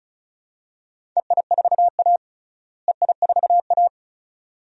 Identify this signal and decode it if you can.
Morse code, heard as a keyed tone, about 36 wpm, 700 Hz. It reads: EI4A EI4A
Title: DXCC-100 Call Signs - 4 Characters - Repeated 2x - 35wpm